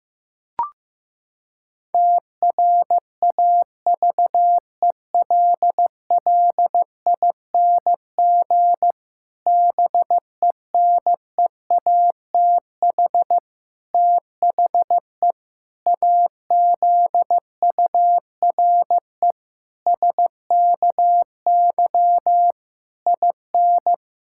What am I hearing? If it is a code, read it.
Morse code, heard as a keyed tone, about 15 wpm, 700 Hz. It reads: TRAVELLING BENEATH THE AZURE SKY IN
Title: Pangram Challenge v2 - 15wpm